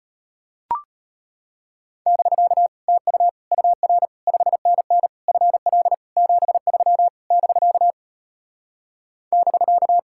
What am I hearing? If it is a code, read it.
Morse code, heard as a keyed tone, about 33 wpm, 700 Hz. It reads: <BK> TU UR 5NN FL 73 <BK> <BK>